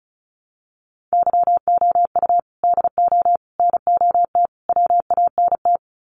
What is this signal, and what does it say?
Morse code, heard as a keyed tone, about 34 wpm, 700 Hz. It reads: YOU DO NOT WANT
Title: Sets of 4 Words 35wpm